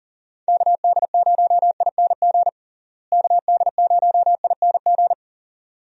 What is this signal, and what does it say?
Morse code, heard as a keyed tone, about 40 wpm, 700 Hz. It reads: KD0ING KD0ING
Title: USA Call Signs - 6 Characters - Repeated 2x - 40wpm